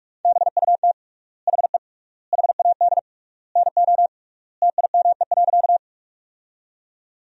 Morse code, heard as a keyed tone, about 46 wpm, 700 Hz. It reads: BUT HE HAD NO TIME.
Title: 3x Word-Spacing - Sentences from Top 100 Words - 45wpm